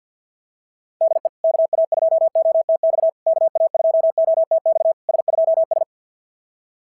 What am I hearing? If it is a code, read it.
Morse code, heard as a keyed tone, about 52 wpm, 690 Hz. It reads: DE KA2OTX KA2OTX S2S
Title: SOTA QSO Practice 50wpm